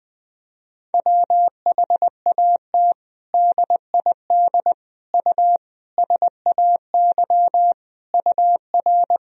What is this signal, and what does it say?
Morse code, heard as a keyed tone, about 20 wpm, 700 Hz. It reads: WHAT DID U SAY UR
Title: Sentences from Top 200 Words and Abbreviations 20wpm